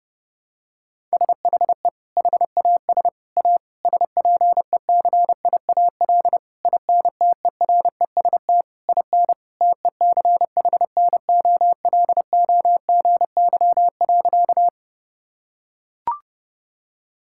Morse code, heard as a keyed tone, about 30 wpm, 700 Hz. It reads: SHE HAS A SPECIAL INTEREST IN TECHNOLOGY.
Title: Sentences from Top 1000 Words - Review Part 5 - 30wpm